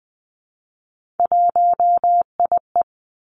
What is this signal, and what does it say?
Morse code, heard as a keyed tone, about 20 wpm, 700 Hz. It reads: 1IE
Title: Three Challenging Letter-Number Combinations 20wpm